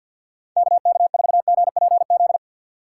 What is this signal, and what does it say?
Morse code, heard as a keyed tone, about 52 wpm, 700 Hz. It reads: KK4GPZ
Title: USA Call Signs - 6 Characters - 6x Character Spacing - 50wpm